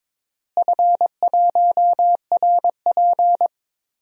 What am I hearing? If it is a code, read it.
Morse code, heard as a keyed tone, about 22 wpm, 700 Hz. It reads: F1RP
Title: DXCC-100 Call Signs - 4 Characters - Repeated 3x - 22wpm